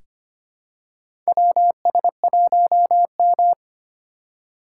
Morse code, heard as a keyed tone, about 25 wpm, 700 Hz. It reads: WS1M
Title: USA Call Signs - 4 Characters - 2x Character Spacing - 25wpm